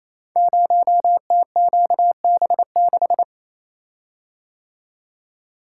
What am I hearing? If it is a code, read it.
Morse code, heard as a keyed tone, about 29 wpm, 700 Hz. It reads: 0TQB6
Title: Code Groups - 5 Letter:Numbers - 28wpm